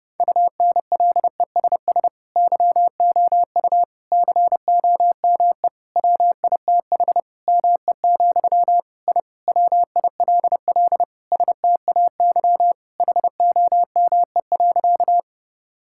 Morse code, heard as a keyed tone, about 32 wpm, 700 Hz. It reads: UNLESS YOU COME WITH ME, I WILL STAY HOME.
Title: Sentences from Top 1000 Words - Review Part 6 - 30wpm